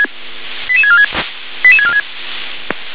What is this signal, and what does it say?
Five-tone selective call (ZVEI), heard as quick runs of five tones, tones of about 70 ms. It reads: 88646 88646